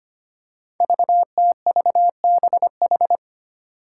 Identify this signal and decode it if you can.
Morse code, heard as a keyed tone, about 25 wpm, 700 Hz. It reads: VTVBH
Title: Five Challenging Letter-Number Combinations 25wpm